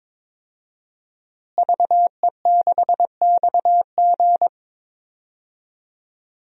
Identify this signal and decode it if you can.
Morse code, heard as a keyed tone, about 22 wpm, 700 Hz. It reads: VE6XG
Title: DXCC-100 Call Signs - 5 Characters - Repeated 2x - 22wpm